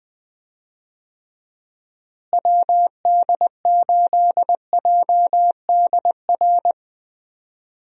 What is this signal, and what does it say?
Morse code, heard as a keyed tone, about 20 wpm, 700 Hz. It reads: WD8JDR
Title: USA Call Signs - 6 Characters - Repeated 3x - 20wpm